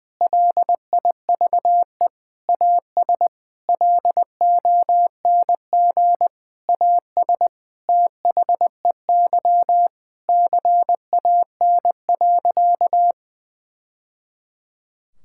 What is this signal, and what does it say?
Morse code, heard as a keyed tone, about 20 wpm, 700 Hz. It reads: LIVE AS LONG AS THEY CAN.
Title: Sentences from Top 1000 Words - Review Part 5 - 20wpm